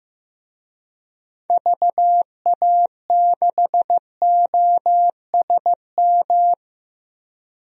Morse code, heard as a keyed tone, about 15 wpm, 700 Hz. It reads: VA6OSM
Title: DXCC-100 Call Signs - 6 Characters - Repeated 2x - 15wpm